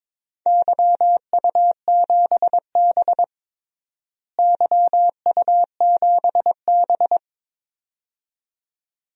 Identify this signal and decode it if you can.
Morse code, heard as a keyed tone, about 22 wpm, 700 Hz. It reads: YU7B YU7B
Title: DXCC-100 Call Signs - 4 Characters - Repeated 2x - 22wpm